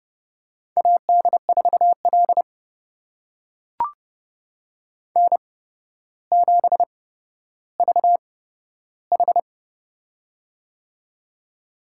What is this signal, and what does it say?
Morse code, heard as a keyed tone, about 30 wpm, spaced as for 13 wpm, 700 Hz. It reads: AD4L N7VH
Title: USA Call Signs - 4 Characters - 8x Character Spacing - 30wpm